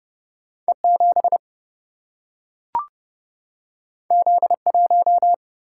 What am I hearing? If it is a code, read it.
Morse code, heard as a keyed tone, about 30 wpm, 700 Hz. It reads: E 7 Z 1
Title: Single-Letter Single-Number 30wpm